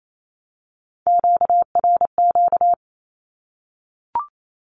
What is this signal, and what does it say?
Morse code, heard as a keyed tone, about 28 wpm, 700 Hz. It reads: QRQ
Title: Q-Codes Repeated 3x 28wpm